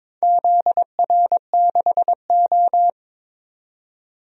Morse code, heard as a keyed tone, about 22 wpm, 700 Hz. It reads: ZR6O
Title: DXCC-100 Call Signs - 4 Characters - 22wpm